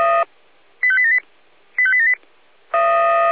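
Five-tone selective call (ZVEI), heard as quick runs of five tones, tones of about 70 ms. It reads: 85878 85878
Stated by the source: Five-tone ZVEI selective call received by SDR.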